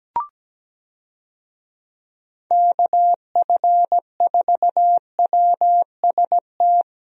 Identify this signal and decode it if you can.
Morse code, heard as a keyed tone, about 17 wpm, 700 Hz. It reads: KF4WST
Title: USA Call Signs - 6 Characters - 17wpm